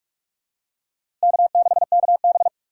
Morse code, heard as a keyed tone, about 44 wpm, 700 Hz. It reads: K6KB
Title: USA Call Signs - 2x Character Spacing - 45wpm